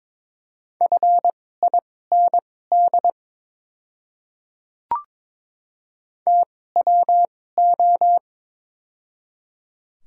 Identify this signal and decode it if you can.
Morse code, heard as a keyed tone, about 22 wpm, spaced as for 16 wpm, 700 Hz. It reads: FIND TWO
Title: 2x Character Spacing - Top 100 Words - 22wpm